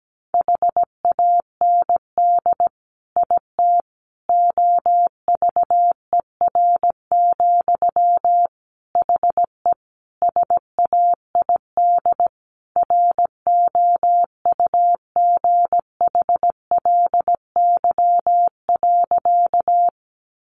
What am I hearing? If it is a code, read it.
Morse code, heard as a keyed tone, about 17 wpm, 700 Hz. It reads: HAND IT OVER, HE SAID ROUGHLY.
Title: Sentences from Top 1500 Words 17wpm